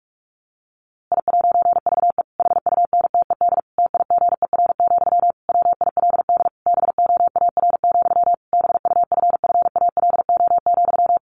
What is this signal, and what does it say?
Morse code, heard as a keyed tone, about 44 wpm, 700 Hz. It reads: I'VE HUNTED TIGER, WILD BOAR, BUFFALO,
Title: A Sound Of Thunder 45wpm